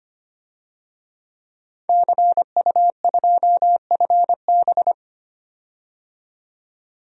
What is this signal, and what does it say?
Morse code, heard as a keyed tone, about 25 wpm, 700 Hz. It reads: CU2FB
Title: DXCC-100 Call Signs - 5 Characters - 25wpm